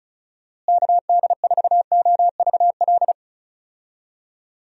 Morse code, heard as a keyed tone, about 34 wpm, 700 Hz. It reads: KD4OVL
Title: USA Call Signs - 6 Characters - Repeated 3x - 35wpm